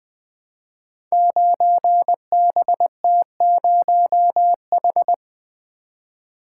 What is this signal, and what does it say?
Morse code, heard as a keyed tone, about 20 wpm, 700 Hz. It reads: 9BT0H
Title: Code Groups - 5 Letter:Numbers - 20wpm